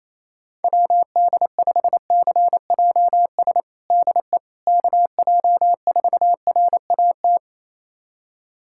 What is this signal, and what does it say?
Morse code, heard as a keyed tone, about 28 wpm, 700 Hz. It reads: WD5CJS DE KJ4RAT
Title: USA Callsign DE Callsign 28wpm